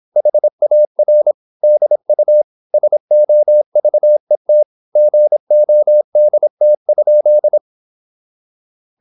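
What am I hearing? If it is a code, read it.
Morse code, heard as a keyed tone, about 26 wpm, 600 Hz. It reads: HAR DU SOVET GODT?
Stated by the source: High-speed (QRQ) Morse at 26 wpm.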